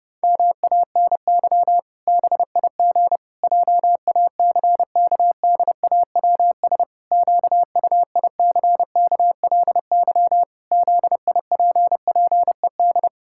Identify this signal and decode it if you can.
Morse code, heard as a keyed tone, about 32 wpm, 700 Hz. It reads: MANY BIG JACKDAWS QUICKLY ZIPPED
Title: Pangram Challenge v2 - 30wpm